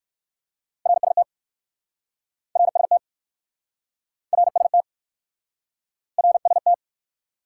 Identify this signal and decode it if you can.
Morse code, heard as a keyed tone, about 60 wpm, 700 Hz. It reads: RST RST RST RST